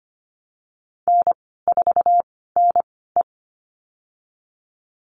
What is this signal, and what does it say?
Morse code, heard as a keyed tone, about 25 wpm, spaced as for 16 wpm, 700 Hz. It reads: N4NE
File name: Lesson 15g - Number 5 - Call Signs 25wpm